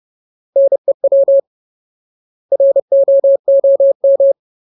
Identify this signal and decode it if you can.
Morse code, heard as a keyed tone, about 30 wpm, 550 Hz. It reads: NEW ROOM
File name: Ultimate Word List / 30WPM Ultimate